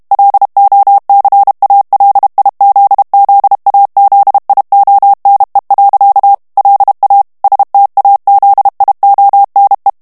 Morse code, heard as a keyed tone, about 32 wpm, 800 Hz. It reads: LOCALIZZAZIONE. LA STAZIONE
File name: ITALIEN texte clair 1 vitesse 30 mots minute